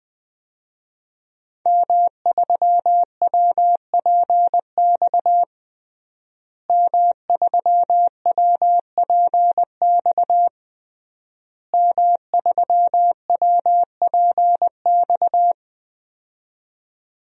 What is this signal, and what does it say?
Morse code, heard as a keyed tone, about 20 wpm, 700 Hz. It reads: M3WPX M3WPX M3WPX
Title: DXCC-100 Call Signs - 5 Characters - Repeated 3x - 20wpm